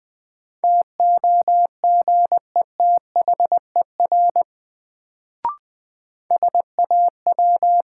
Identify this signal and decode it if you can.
Morse code, heard as a keyed tone, about 21 wpm, 700 Hz. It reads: TOGETHER SAW
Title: Sentences from Top 300 Words - New Word Drill - 20wpm